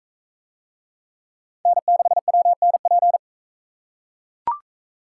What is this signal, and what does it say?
Morse code, heard as a keyed tone, about 43 wpm, 700 Hz. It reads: N6WNP